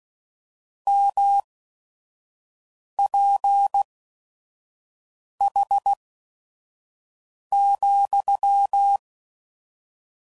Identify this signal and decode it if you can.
Morse code, heard as a keyed tone, about 16 wpm, spaced as for 5 wpm, 800 Hz. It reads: MPH,
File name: CW Code Groups @5WPM / Weather Sets / Weather 5 5